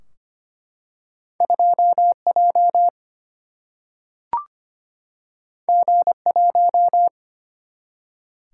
Morse code, heard as a keyed tone, about 25 wpm, 700 Hz. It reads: 2J G1
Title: Single-Letter Single-Number 25wpm